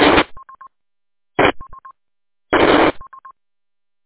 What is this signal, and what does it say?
There is no code.